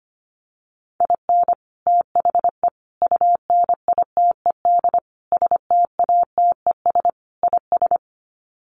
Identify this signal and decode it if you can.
Morse code, heard as a keyed tone, about 25 wpm, 700 Hz. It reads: IN THE UNITED STATES IS
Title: Sets of 5 Words 25wpm